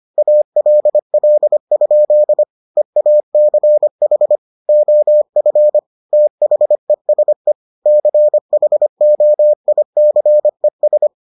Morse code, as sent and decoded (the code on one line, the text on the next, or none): .- .-.. .-.. ..--.. / . .- -.-. .... / --- ..-. / - .... ... / -.-. .... --- .. -.-. ...
ALL? EACH OF THESE CHOICES